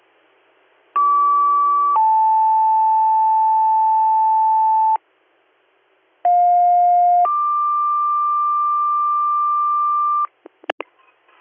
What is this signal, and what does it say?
Two-tone page, heard as a one-second tone, then a three-second tone, about 1150 then 880 Hz, about 710 then 1180 Hz. It is heard from 1.0 to 10.3 s.